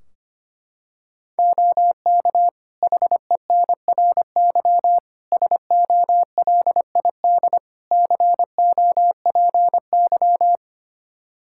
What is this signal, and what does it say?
Morse code, heard as a keyed tone, about 25 wpm, 700 Hz. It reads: OK HENRY SOLID COPY
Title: Practice QSO - Exchange 3 - More Info - 25wpm